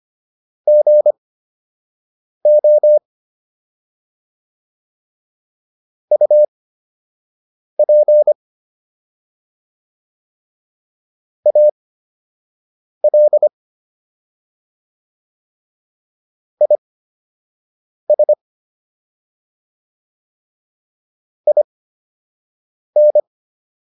Morse code, heard as a keyed tone, about 25 wpm, spaced as for 6 wpm, 600 Hz. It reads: GO UP AL IS IN